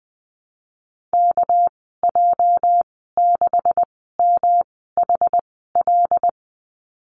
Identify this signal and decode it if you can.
Morse code, heard as a keyed tone, about 20 wpm, spaced as for 15 wpm, 700 Hz. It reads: KJ6MHL
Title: 207 Lesson 33g - Letter j - Call Signs 20wpm